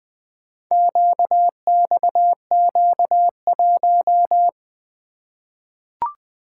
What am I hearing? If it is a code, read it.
Morse code, heard as a keyed tone, about 20 wpm, 700 Hz. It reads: QXQ1
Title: Four Letter-Number Combinations - 20wpm